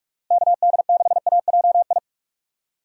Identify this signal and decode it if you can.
Morse code, heard as a keyed tone, about 41 wpm, 700 Hz. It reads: KD6AJI
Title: USA Call Signs - 6 Characters - Repeated 2x - 45wpm